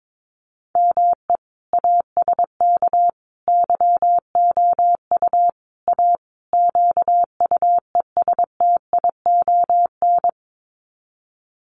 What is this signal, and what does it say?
Morse code, heard as a keyed tone, about 22 wpm, 700 Hz. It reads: ME ASK YOU A QUESTION
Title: Sets of 5 Words Encore2 22wpm